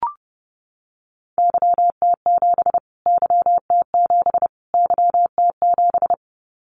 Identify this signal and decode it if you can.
Morse code, heard as a keyed tone, about 30 wpm, 700 Hz. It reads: YT7 YT7 YT7
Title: Callsign Prefixes for Top 25 DXCC - Repeated 3x - 30wpm